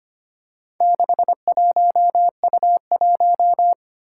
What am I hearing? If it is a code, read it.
Morse code, heard as a keyed tone, about 25 wpm, 700 Hz. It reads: 61U1
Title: Four Challenging Letter-Number Combinations 25wpm